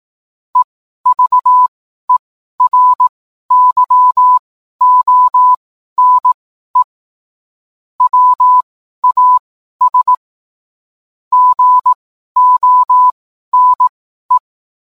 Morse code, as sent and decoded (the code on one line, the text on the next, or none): . ...- . .-. -.-- --- -. / .-- .- ... / --. --- -. .
EVERYONE WAS GONE